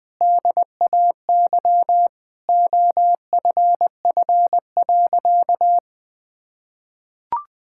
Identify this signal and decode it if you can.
Morse code, heard as a keyed tone, about 20 wpm, 700 Hz. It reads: DAY OFF.
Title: Sentences with 7-Word Phrases - 20wpm